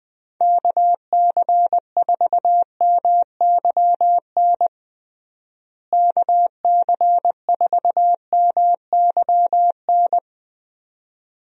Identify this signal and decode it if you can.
Morse code, heard as a keyed tone, about 21 wpm, 700 Hz. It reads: KC4MYN KC4MYN
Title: USA Call Signs - 6 Characters - Repeated 2x - 20wpm